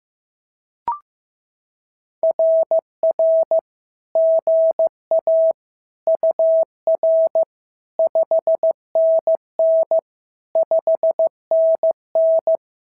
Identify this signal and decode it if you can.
Morse code, heard as a keyed tone, about 14 wpm, 650 Hz. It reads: RR GA UR 5NN 5NN
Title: SOTA QSO Practice 15wpm